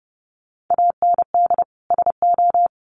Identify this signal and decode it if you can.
Morse code, heard as a keyed tone, about 30 wpm, 700 Hz. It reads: AND SO
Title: Sentences from Top 1500 Words 30wpm